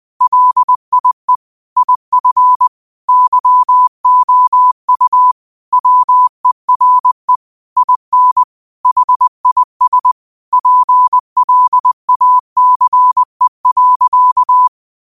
Morse code, heard as a keyed tone, about 20 wpm, 1000 Hz. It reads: LIE IF YOU WERE IN HIS PLACE.
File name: qotdfiles / 20wpm